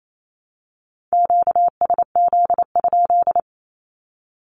Morse code, heard as a keyed tone, about 28 wpm, 700 Hz. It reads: QSZ?
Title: Q-Codes Repeated 3x 28wpm